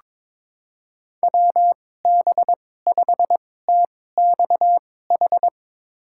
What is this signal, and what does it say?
Morse code, heard as a keyed tone, about 22 wpm, spaced as for 16 wpm, 700 Hz. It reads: WB5TXH
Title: USA Call Signs - 2x Character Spacing - 22wpm